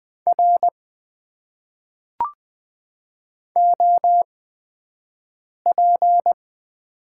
Morse code, heard as a keyed tone, about 20 wpm, spaced as for 5 wpm, 700 Hz. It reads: R OP
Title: Top 90 QSO Elements - 8x Character Spacing - 20wpm